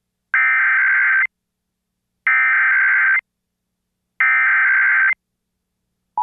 Emergency Alert System (SAME) tones (header bursts), heard as harsh buzzing bursts. It is heard from 0.3 to 5.1 s.